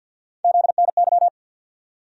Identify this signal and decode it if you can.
Morse code, heard as a keyed tone, about 51 wpm, 700 Hz. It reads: ZNY
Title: USA Call Sign Postfixes - 3 Letters 50wpm